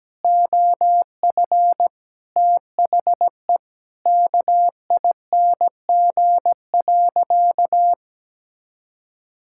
Morse code, heard as a keyed tone, about 17 wpm, 700 Hz. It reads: OF THE KING.
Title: Sentences from Top 1000 Words - Review Part 6 - 17wpm